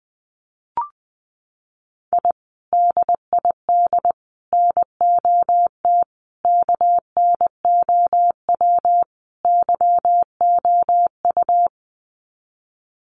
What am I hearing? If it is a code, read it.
Morse code, heard as a keyed tone, about 20 wpm, 700 Hz. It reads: I DID NOT KNOW YOU
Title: Sets of 5 Words Encore2 20wpm